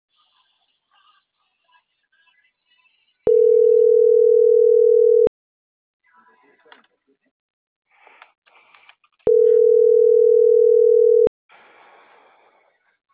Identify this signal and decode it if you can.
Ringback tone (North America), 440 + 480 Hz.
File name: CENTRAL MABE / out noemi.gutierrez